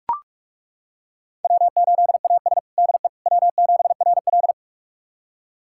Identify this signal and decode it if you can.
Morse code, heard as a keyed tone, about 42 wpm, 700 Hz. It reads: W8AS DE W7RL
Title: USA Callsign DE Callsign - 4 Characters - 45wpm